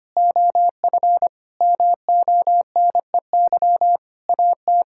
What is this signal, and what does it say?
Morse code, heard as a keyed tone, about 25 wpm, 700 Hz. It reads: OF MONEY AT
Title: Sentences from Top 1000 Words - Review Part 6 - 25wpm